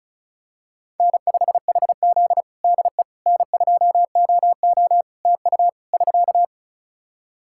Morse code, heard as a keyed tone, about 34 wpm, 700 Hz. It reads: N5HZ DE N2OO TU <SK>
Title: Practice QSO - Exchange 4 - Ending - 35wpm